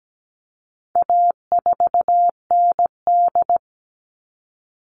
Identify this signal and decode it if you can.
Morse code, heard as a keyed tone, about 17 wpm, 700 Hz. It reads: A4ND
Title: All Cut Numbers - 4-Digit 17wpm